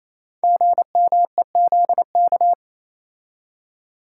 Morse code, heard as a keyed tone, about 28 wpm, 700 Hz. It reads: GMEZK
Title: Code Groups - 5 Letter:Numbers - 28wpm